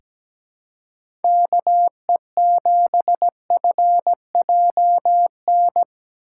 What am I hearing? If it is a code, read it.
Morse code, heard as a keyed tone, about 17 wpm, 700 Hz. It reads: KE7FJN
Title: USA Call Signs - 6 Characters - 3x Character Spacing - 17wpm